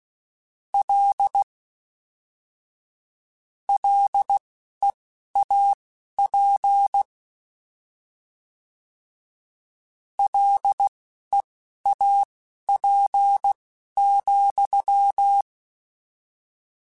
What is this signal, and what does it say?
Morse code, heard as a keyed tone, about 16 wpm, spaced as for 12 wpm, 800 Hz. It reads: L LEAP LEAP,
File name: Combinations / Combinations-mp3s / Combinations@10